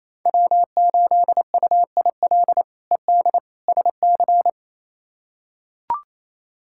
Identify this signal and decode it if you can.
Morse code, heard as a keyed tone, about 28 wpm, 700 Hz. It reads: W8UIL ED SC
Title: K1USN SST QSO Practice - 28wpm